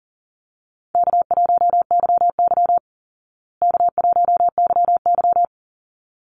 Morse code, heard as a keyed tone, about 40 wpm, 700 Hz. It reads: K1YY K1YY
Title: Easy Call Signs Repeated 2x - 40wpm